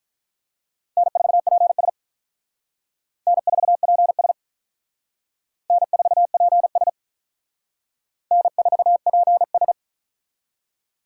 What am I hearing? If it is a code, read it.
Morse code, heard as a keyed tone, about 45 wpm, 700 Hz. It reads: N4PS N4PS N4PS N4PS